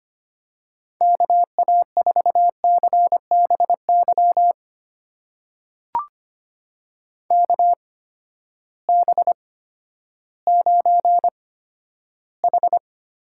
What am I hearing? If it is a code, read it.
Morse code, heard as a keyed tone, about 25 wpm, spaced as for 12 wpm, 700 Hz. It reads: KA4CBY KB9H
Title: USA Call Signs - 6 Characters - 8x Character Spacing - 25wpm